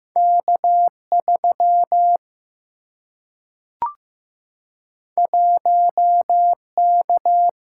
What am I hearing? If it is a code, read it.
Morse code, heard as a keyed tone, about 15 wpm, 700 Hz. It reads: K3 1K
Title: Single-Letter Single-Number 15wpm